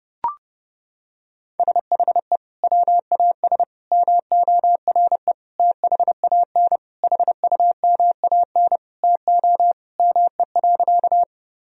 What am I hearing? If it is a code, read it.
Morse code, heard as a keyed tone, about 30 wpm, 700 Hz. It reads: SHE WAS MORE THAN HUMAN TO ME.
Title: Sentences from Top 1000 Words - Review Part 3 - 30wpm